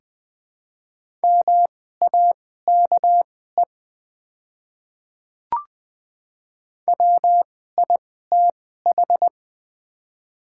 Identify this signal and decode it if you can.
Morse code, heard as a keyed tone, about 20 wpm, spaced as for 14 wpm, 690 Hz. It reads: MAKE WITH
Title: 2x Character Spacing - Top 100 Words - 20wpm